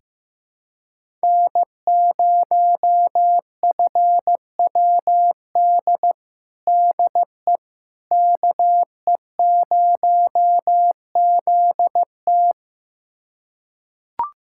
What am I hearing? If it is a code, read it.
Morse code, heard as a keyed tone, about 15 wpm, 700 Hz. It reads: N0FWD DE KE0ZT
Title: USA Callsign DE Callsign - 5 Characters - 15wpm